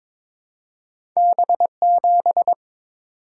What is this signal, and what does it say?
Morse code, heard as a keyed tone, about 22 wpm, 700 Hz. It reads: B7
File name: practice / 22wpm / Two Letter-Number Combinations - 22wpm